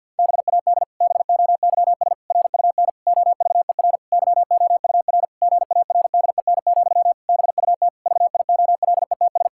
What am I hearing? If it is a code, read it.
Morse code, heard as a keyed tone, about 51 wpm, 700 Hz. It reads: BAD BOYS RUN OVER YOUR GARDEN, BUT VIOLETS
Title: Pangram Challenge v2 - 50wpm